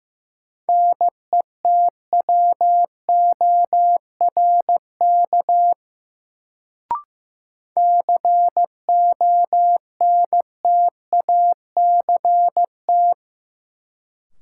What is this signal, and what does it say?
Morse code, heard as a keyed tone, about 15 wpm, 700 Hz. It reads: NETWORK CONTACT
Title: Sentences from Top 1000 Words - Review Part 7 - New Word Drill - 15wpm